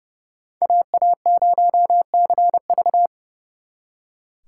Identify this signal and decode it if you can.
Morse code, heard as a keyed tone, about 36 wpm, 700 Hz. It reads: AA0CV
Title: Speed Racing - USA 2x2 Call Signs 30wpm